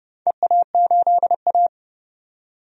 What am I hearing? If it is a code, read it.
Morse code, heard as a keyed tone, about 30 wpm, 700 Hz. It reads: EA8A
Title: DXCC-100 Call Signs - 4 Characters - 30wpm